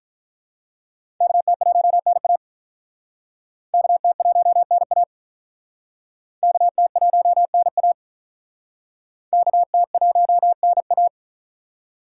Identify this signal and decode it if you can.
Morse code, heard as a keyed tone, about 46 wpm, 700 Hz. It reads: KT1NA KT1NA KT1NA KT1NA